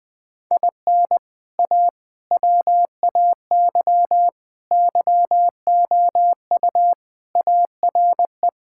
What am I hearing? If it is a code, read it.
Morse code, heard as a keyed tone, about 20 wpm, 700 Hz. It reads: IN A WAY YOU ARE
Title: Sentences with 3-Word Phrases - 20wpm